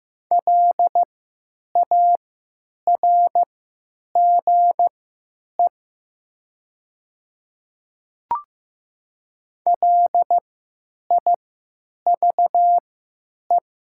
Morse code, heard as a keyed tone, about 15 wpm, spaced as for 8 wpm, 700 Hz. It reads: LARGE LIVE
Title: 3x Character Spacing - Top 300 Words - 15wpm